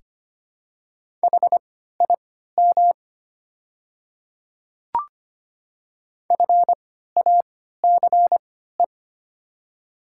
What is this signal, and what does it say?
Morse code, heard as a keyed tone, about 25 wpm, spaced as for 14 wpm, 700 Hz. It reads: HIM FACE
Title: 3x Character Spacing - Top 300 Words - 25wpm